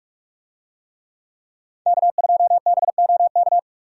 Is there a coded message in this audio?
Yes: Morse code from 1.9 to 3.6 s, heard as a keyed tone, about 46 wpm, 700 Hz. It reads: K2BOK